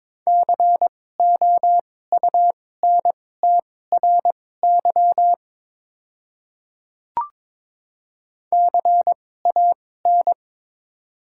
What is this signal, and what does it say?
Morse code, heard as a keyed tone, about 22 wpm, spaced as for 16 wpm, 700 Hz. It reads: COUNTRY CAN
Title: 2x Character Spacing - Top 300 Words - 22wpm